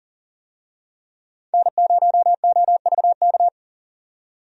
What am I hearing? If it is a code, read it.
Morse code, heard as a keyed tone, about 40 wpm, 700 Hz. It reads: N0OVK
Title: USA Call Signs - 5 Characters - 40wpm